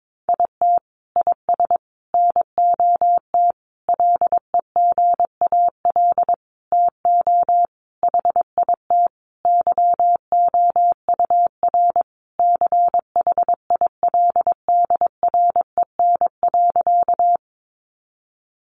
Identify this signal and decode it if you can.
Morse code, heard as a keyed tone, about 22 wpm, 700 Hz. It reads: IT IS NOT LEGAL TO HIT YOUR CHILDREN.
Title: Sentences from Top 1200 Words 22wpm